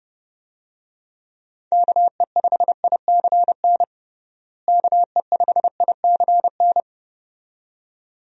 Morse code, heard as a keyed tone, about 30 wpm, 700 Hz. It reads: KE5ICN KE5ICN
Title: USA Call Signs - 6 Characters - Repeated 2x - 30wpm